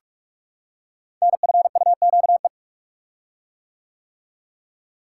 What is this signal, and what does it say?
Morse code, heard as a keyed tone, about 44 wpm, 700 Hz. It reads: NFUQE
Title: Code Groups - 5 Letter:Numbers - 45wpm